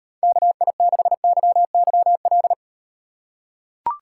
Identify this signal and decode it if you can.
Morse code, heard as a keyed tone, about 38 wpm, 700 Hz. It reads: KI6YYL